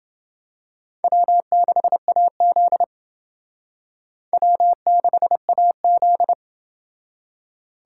Morse code, heard as a keyed tone, about 28 wpm, 700 Hz. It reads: W6AZ W6AZ